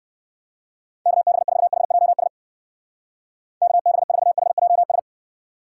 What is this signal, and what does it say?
Morse code, heard as a keyed tone, about 60 wpm, 700 Hz. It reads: KB4HJS KB4HJS